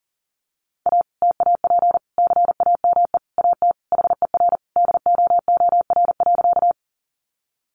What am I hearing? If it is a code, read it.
Morse code, heard as a keyed tone, about 40 wpm, 700 Hz. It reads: A TAP CAME AT HER DOOR.
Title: Sentences from Top 1400 Words 40wpm